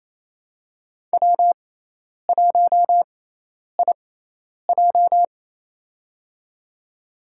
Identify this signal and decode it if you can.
Morse code, heard as a keyed tone, about 28 wpm, spaced as for 10 wpm, 700 Hz. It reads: W1IJ
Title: USA Call Signs - 6x Character Spacing - 28wpm